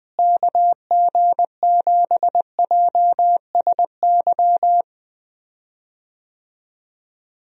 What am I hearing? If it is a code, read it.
Morse code, heard as a keyed tone, about 20 wpm, 700 Hz. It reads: KG7JSY
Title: USA Call Signs - 6 Characters - Repeated 2x - 20wpm